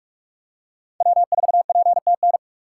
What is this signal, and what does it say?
Morse code, heard as a keyed tone, about 46 wpm, 690 Hz. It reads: W4PTN
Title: USA Call Signs - 6x Character Spacing - 45wpm